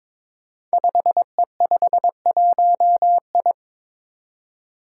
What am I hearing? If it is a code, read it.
Morse code, heard as a keyed tone, about 22 wpm, 700 Hz. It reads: 5E51I
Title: Five Challenging Letter-Number Combinations 22wpm